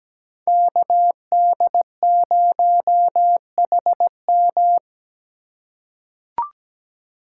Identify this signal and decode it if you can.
Morse code, heard as a keyed tone, about 17 wpm, 700 Hz. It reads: KD0HM
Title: USA Call Signs - 5 Characters - 2x Character Spacing - 17wpm